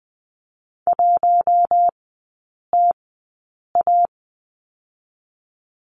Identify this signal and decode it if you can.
Morse code, heard as a keyed tone, about 20 wpm, spaced as for 8 wpm, 700 Hz. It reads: 1TA
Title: 138 Lesson 23e - Letter f - Three Characters 20wpm